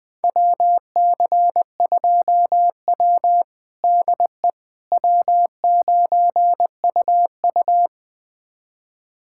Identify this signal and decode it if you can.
Morse code, heard as a keyed tone, about 21 wpm, 700 Hz. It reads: WC2W DE W9UU
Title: USA Callsign DE Callsign - 4 Characters - 20wpm